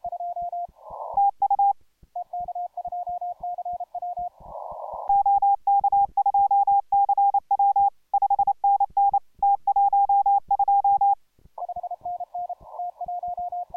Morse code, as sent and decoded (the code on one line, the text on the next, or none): - ..- / --- -.- ..--- -.-. .-- / ..... -. -. / - .---- ..---
TU OK2CW 5NN T12